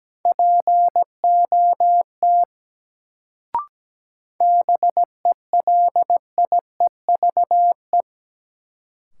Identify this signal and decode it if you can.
Morse code, heard as a keyed tone, about 17 wpm, 700 Hz. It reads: POT BELIEVE
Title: Sentences from Top 1000 Words - New Word Drill - 17wpm